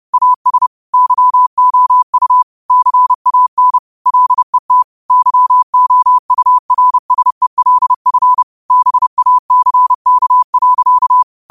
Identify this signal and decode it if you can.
Morse code, heard as a keyed tone, about 30 wpm, 1000 Hz. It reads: AS YOU CAN LET YOURSELF BACK.
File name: qotdfiles / 30wpm